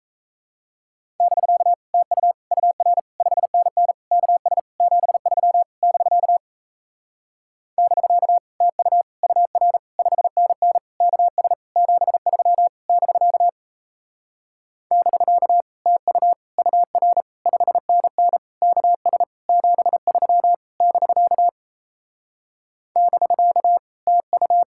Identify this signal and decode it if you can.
Morse code, heard as a keyed tone, about 38 wpm, 700 Hz. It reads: <BK> TU UR 5NN KS 73 <BK> <BK> TU UR 5NN KS 73 <BK> <BK> TU UR 5NN KS 73 <BK> <BK> TU